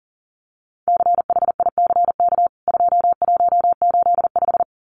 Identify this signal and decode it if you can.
Morse code, heard as a keyed tone, about 40 wpm, 700 Hz. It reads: CHICK 2185
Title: CWT Exchange 40wpm